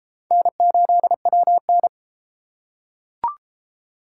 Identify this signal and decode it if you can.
Morse code, heard as a keyed tone, about 31 wpm, 700 Hz. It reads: N8WN